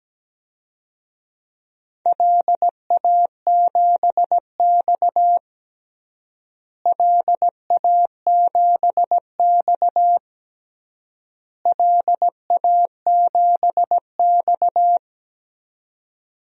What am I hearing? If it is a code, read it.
Morse code, heard as a keyed tone, about 17 wpm, 700 Hz. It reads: LA7X LA7X LA7X
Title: DXCC-100 Call Signs - 4 Characters - Repeated 3x - 17wpm